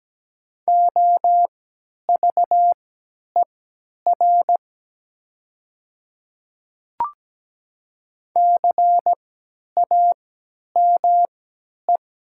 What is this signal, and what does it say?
Morse code, heard as a keyed tone, about 17 wpm, spaced as for 10 wpm, 700 Hz. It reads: OVER CAME
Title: 3x Character Spacing - Top 300 Words - 17wpm